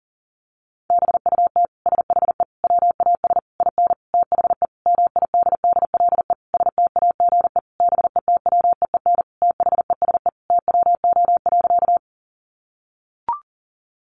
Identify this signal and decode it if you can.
Morse code, heard as a keyed tone, about 37 wpm, 700 Hz. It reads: BUT SHE WAS IN THE MIDDLE STAGE BETWEEN THESE TWO.
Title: Sentences from Top 1200 Words 40wpm